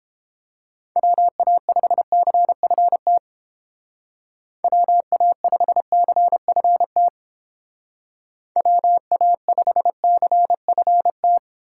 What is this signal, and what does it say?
Morse code, heard as a keyed tone, about 30 wpm, 700 Hz. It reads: WA5CFT WA5CFT WA5CFT